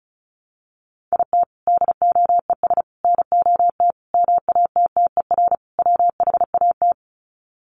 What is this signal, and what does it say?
Morse code, heard as a keyed tone, about 34 wpm, 700 Hz. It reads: IT DOES NOT MATTER WHAT
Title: Sets of 5 Words Encore 35wpm